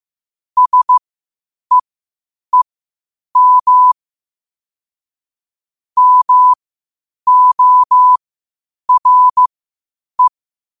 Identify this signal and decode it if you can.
Morse code, heard as a keyed tone, about 15 wpm, spaced as for 8 wpm, 1000 Hz. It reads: SEEM MORE
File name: qotdfiles / 8wpm